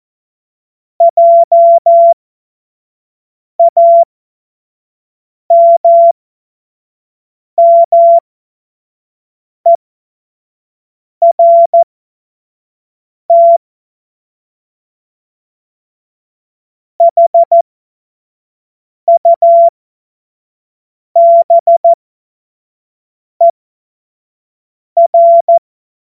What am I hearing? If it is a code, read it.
Morse code, heard as a keyed tone, about 14 wpm, spaced as for 5 wpm, 680 Hz. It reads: JAMMERT HUBER
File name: Morsekurs / Witze / Witz 22. Februar 2022 25 F70